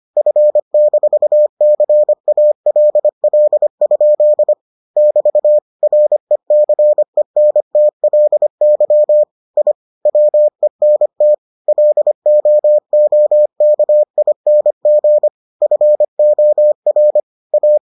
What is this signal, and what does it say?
Morse code, heard as a keyed tone, about 25 wpm, 600 Hz. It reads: F-CALL? <BT> RECENTLY I WENT LOOKING FOR A